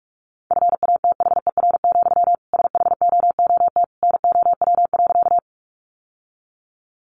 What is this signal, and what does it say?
Morse code, heard as a keyed tone, about 44 wpm, 700 Hz. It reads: FATHER, SHOOT NOW.
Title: Sentences from Top 1200 Words 45wpm